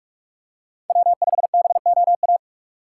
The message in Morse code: W5BOA

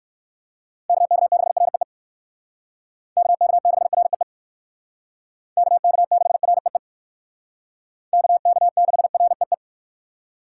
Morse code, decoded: KK6REE KK6REE KK6REE KK6REE

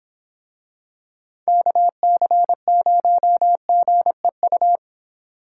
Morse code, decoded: KC0GEU

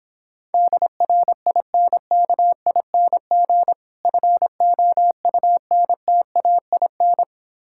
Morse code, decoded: DRINKING FOUNTAIN

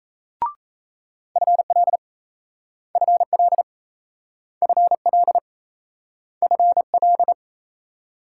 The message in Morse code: FL FL FL FL

